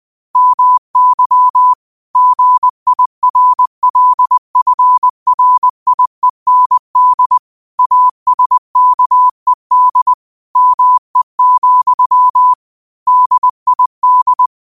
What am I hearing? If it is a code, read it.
Morse code, heard as a keyed tone, about 20 wpm, 1000 Hz. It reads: MY GIRLFRIEND ASKED ME, DID
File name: qotdfiles / 20wpm